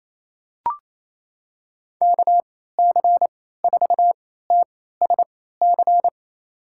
Morse code, heard as a keyed tone, about 28 wpm, spaced as for 16 wpm, 700 Hz. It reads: KC4TSC
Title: USA Call Signs - 6 Characters - 3x Character Spacing - 28wpm